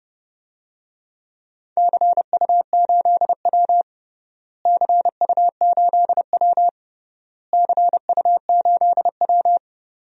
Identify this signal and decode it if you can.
Morse code, heard as a keyed tone, about 32 wpm, 700 Hz. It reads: CU8W CU8W CU8W
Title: DXCC-100 Call Signs - 4 Characters - Repeated 3x - 30wpm